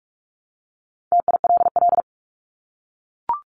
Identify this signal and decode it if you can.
Morse code, heard as a keyed tone, about 44 wpm, 700 Hz. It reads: TILL
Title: Top 1751-2000 Words 45wpm